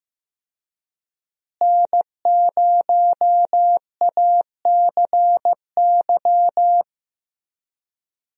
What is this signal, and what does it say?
Morse code, heard as a keyed tone, about 15 wpm, 700 Hz. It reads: N0ACY
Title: USA Call Signs - 5 Characters - Repeated 2x - 15wpm